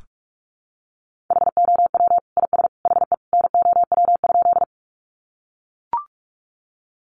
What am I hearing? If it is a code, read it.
Morse code, heard as a keyed tone, about 44 wpm, 700 Hz. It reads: HOW IS HE NOW?
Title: Sentences from Top 100 Words 45wpm